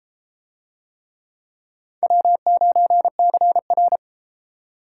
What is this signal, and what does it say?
Morse code, heard as a keyed tone, about 29 wpm, 700 Hz. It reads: W9CR